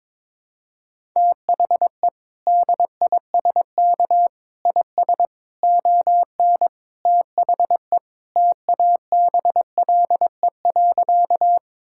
Morse code, heard as a keyed tone, about 22 wpm, 700 Hz. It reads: THE DISK IS ON THE TABLE.
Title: Sentences from Top 1000 Words - Review Part 6 - 22wpm